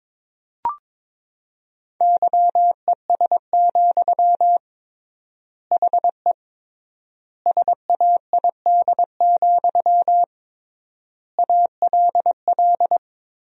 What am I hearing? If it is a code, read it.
Morse code, heard as a keyed tone, about 22 wpm, 700 Hz. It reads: YES, HE SAID, ALL
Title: 3x Word-Spacing - Sentences from Top 500 Words - 22wpm